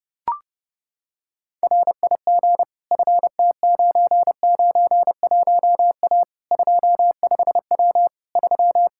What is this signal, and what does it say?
Morse code, heard as a keyed tone, about 30 wpm, 700 Hz. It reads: RIG FT991A 25W 3
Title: K1USN SST QSO Practice 30wpm